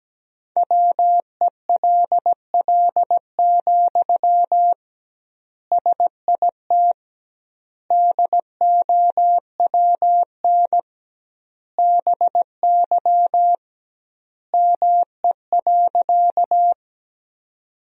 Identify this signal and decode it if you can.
Morse code, heard as a keyed tone, about 17 wpm, 700 Hz. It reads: WELL, SIT DOWN BY ME.
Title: 2x Word-Spacing - Sentences from Top 500 Words - 17wpm